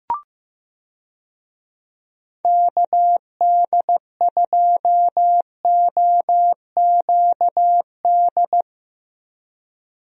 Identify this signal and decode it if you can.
Morse code, heard as a keyed tone, about 15 wpm, 700 Hz. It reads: KD2OQD
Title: USA Call Signs - 6 Characters - Repeated 3x - 15wpm